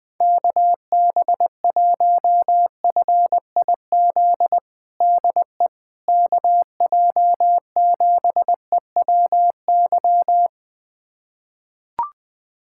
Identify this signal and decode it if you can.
Morse code, heard as a keyed tone, about 20 wpm, 700 Hz. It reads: KB1FIZ DE KJ7EWY
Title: USA Callsign DE Callsign 20wpm